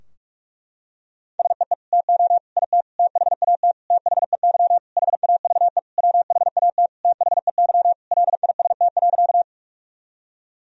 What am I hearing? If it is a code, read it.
Morse code, heard as a keyed tone, about 46 wpm, 700 Hz. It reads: SEE TO IT THAT THEY HAVE WHAT THEY LIST.
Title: Sentences from Top 1000 Words - Review Part 2 - 45wpm